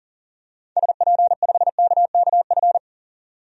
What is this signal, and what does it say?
Morse code, heard as a keyed tone, about 40 wpm, 700 Hz. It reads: SP5KKF
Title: DXCC-100 Call Signs - 6 Characters - 40wpm